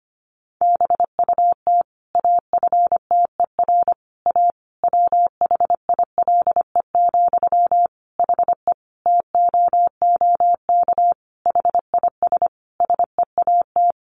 Morse code, heard as a keyed tone, about 25 wpm, 700 Hz. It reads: BUT AFTER A WHILE, HE TOOK HIS SEAT
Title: Sentences from Top 1500 Words 25wpm